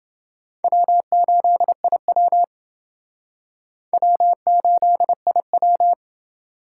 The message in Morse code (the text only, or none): W8IW W8IW